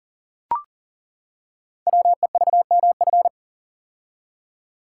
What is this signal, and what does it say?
Morse code, heard as a keyed tone, about 42 wpm, 690 Hz. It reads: WEVMF
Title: Code Groups - 5 Letters - 40wpm